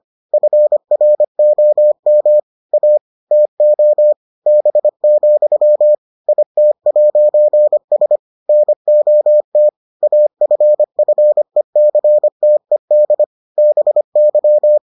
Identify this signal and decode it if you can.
Morse code, heard as a keyed tone, about 25 wpm, 600 Hz. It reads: FROM A TO B, IT'S NOT AFFECTED BY